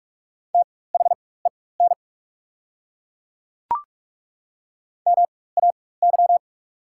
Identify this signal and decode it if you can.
Morse code, heard as a keyed tone, about 43 wpm, spaced as for 21 wpm, 690 Hz. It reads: THEN MAY
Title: 4x Character Spacing - Top 300 Words - 45wpm